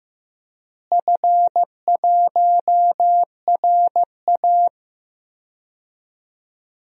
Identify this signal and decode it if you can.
Morse code, heard as a keyed tone, about 15 wpm, 700 Hz. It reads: F1RA
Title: DXCC-100 Call Signs - 4 Characters - Repeated 3x - 15wpm